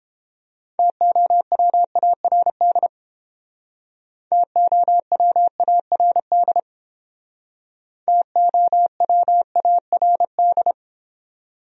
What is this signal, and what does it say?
Morse code, heard as a keyed tone, about 30 wpm, 700 Hz. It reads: TOWARD TOWARD TOWARD